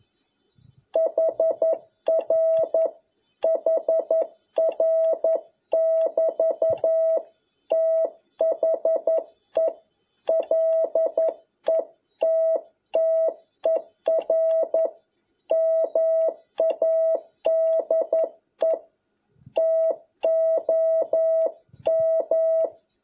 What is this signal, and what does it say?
Morse code, heard as a keyed tone, about 11 wpm, 650 Hz. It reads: HR HR <BT> THE LETTER MADE TOM
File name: jh8cbh / 2021cw